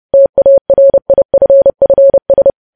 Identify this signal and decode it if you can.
Morse code, heard as a keyed tone, about 30 wpm, 550 Hz. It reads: TARIFFS
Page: news headlines in morse code at 30 wpm